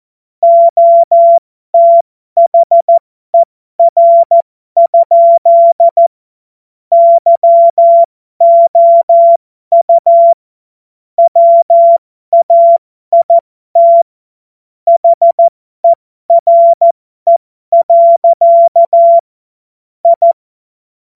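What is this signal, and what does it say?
Morse code, heard as a keyed tone, about 14 wpm, spaced as for 12 wpm, 680 Hz. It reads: OTHER? YOU WAIT HERE. I
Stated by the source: Morse code joke.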